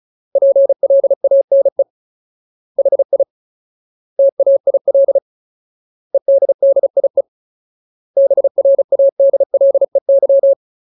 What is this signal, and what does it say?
Morse code, heard as a keyed tone, about 34 wpm, 550 Hz. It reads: PLANE HI TAIL EDDIE BRADLEY
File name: Ultimate Word List / 35WPM Ultimate